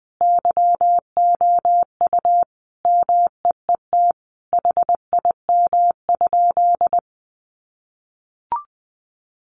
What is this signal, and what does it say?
Morse code, heard as a keyed tone, about 20 wpm, 700 Hz. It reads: YOU MEET HIM?
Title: Sentences from Top 1100 Words 20wpm